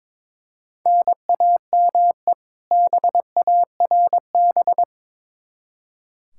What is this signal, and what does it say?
Morse code, heard as a keyed tone, about 22 wpm, 700 Hz. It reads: NAME BARB
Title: K1USN SST QSO Practice 22wpm